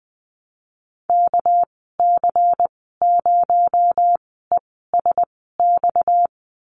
Morse code, heard as a keyed tone, about 20 wpm, spaced as for 15 wpm, 700 Hz. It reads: KC0ESX
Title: 231 Lesson 37g - Letter x - Call Signs 20wpm